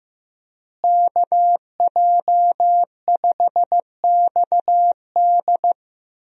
Morse code, heard as a keyed tone, about 15 wpm, 700 Hz. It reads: KJ5XD
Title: USA Call Signs - 5 Characters - Repeated 2x - 15wpm